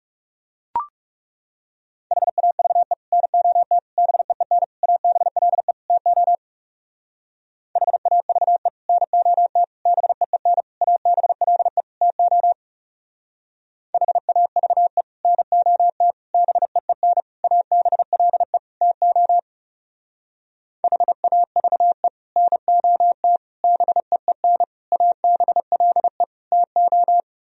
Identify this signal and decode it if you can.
Morse code, heard as a keyed tone, about 38 wpm, 700 Hz. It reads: HAVE NOT BEEN ABLE TO HAVE NOT BEEN ABLE TO HAVE NOT BEEN ABLE TO HAVE NOT BEEN ABLE TO